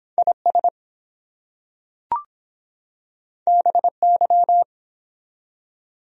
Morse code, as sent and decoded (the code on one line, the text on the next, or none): .. ... / -... -.--
IS BY